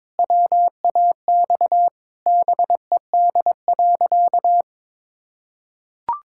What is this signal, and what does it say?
Morse code, heard as a keyed tone, about 22 wpm, 700 Hz. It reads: WAX BED.
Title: Pangram Challenge v2 - 22wpm